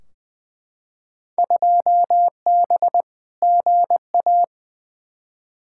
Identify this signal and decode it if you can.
Morse code, heard as a keyed tone, about 20 wpm, 700 Hz. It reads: 2B GA
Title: Field Day Exchanges Encore - 20wpm